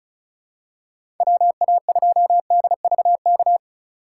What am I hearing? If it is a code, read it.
Morse code, heard as a keyed tone, about 34 wpm, 700 Hz. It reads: WA2DVK
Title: USA Call Signs - 6 Characters - 4x Character Spacing - 35wpm